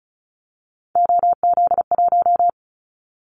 Morse code, heard as a keyed tone, about 36 wpm, 700 Hz. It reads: OZ1
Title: Callsign Prefixes for Top 100 DXCC - 35wpm